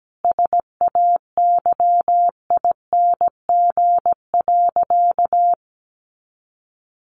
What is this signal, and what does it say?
Morse code, heard as a keyed tone, about 17 wpm, 700 Hz. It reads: SAYING.
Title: A Sound Of Thunder 17wpm